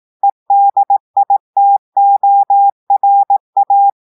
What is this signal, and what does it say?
Morse code, heard as a keyed tone, about 18 wpm, 800 Hz. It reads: EDITORA
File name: audios / portugais / portugais texte en clair 1 vitesse 18 mots minutes